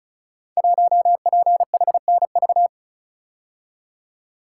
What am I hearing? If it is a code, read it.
Morse code, heard as a keyed tone, about 36 wpm, 700 Hz. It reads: 1PHNV
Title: Code Groups - 5 Letter:Numbers - 35wpm